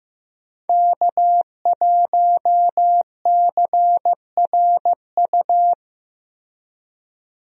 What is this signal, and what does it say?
Morse code, heard as a keyed tone, about 15 wpm, 700 Hz. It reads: K1CRU
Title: USA Call Signs - 5 Characters - Repeated 3x - 15wpm